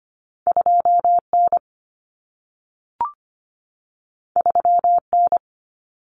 Morse code, heard as a keyed tone, about 25 wpm, 700 Hz. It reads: 2N 3N
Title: Common Cut Numbers - 2-Digit 25wpm